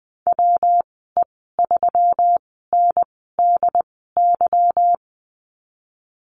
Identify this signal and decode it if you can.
Morse code, heard as a keyed tone, about 20 wpm, spaced as for 15 wpm, 700 Hz. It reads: WE3NDY
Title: 154 Lesson 25g - Letter p - Call Signs 20wpm